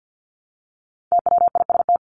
Morse code, heard as a keyed tone, about 50 wpm, 700 Hz. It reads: TWIST
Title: Top 1751-2000 Words 50wpm